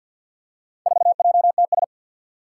Morse code, heard as a keyed tone, about 49 wpm, 700 Hz. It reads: 4JTS
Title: Four Challenging Letter-Number Combinations 50wpm